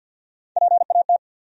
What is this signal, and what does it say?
Morse code, heard as a keyed tone, about 50 wpm, 700 Hz. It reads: PAT
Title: USA Common Names - Max 3 Letters - 50wpm